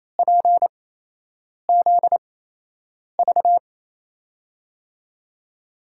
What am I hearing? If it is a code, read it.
Morse code, heard as a keyed tone, about 28 wpm, spaced as for 8 wpm, 700 Hz. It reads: PZV
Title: USA Call Signs - 8x Character Spacing - 28wpm